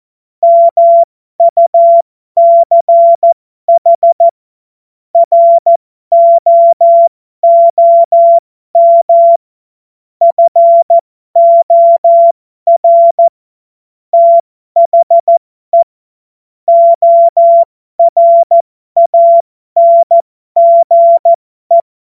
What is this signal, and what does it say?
Morse code, heard as a keyed tone, about 14 wpm, spaced as for 12 wpm, 680 Hz. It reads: MUCH ROOM FOR THE ORANGE
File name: Morsekurs / Witze